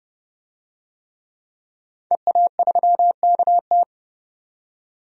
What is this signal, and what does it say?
Morse code, heard as a keyed tone, about 30 wpm, 700 Hz. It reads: EA3KT
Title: DXCC-100 Call Signs - 5 Characters - 30wpm